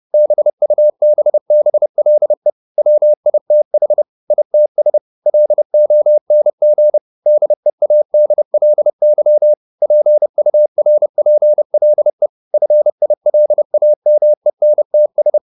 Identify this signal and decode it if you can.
Morse code, heard as a keyed tone, about 30 wpm, 600 Hz. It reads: BUBBLE WITH ITS LONG DEADLY PURPLE FILAMENTS
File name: the old man and the sea / the old man and the sea 1 30wpm